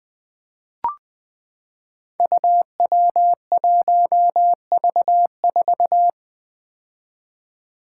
Morse code, heard as a keyed tone, about 20 wpm, 700 Hz. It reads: UW1V4
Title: Five Challenging Letter-Number Combinations 20wpm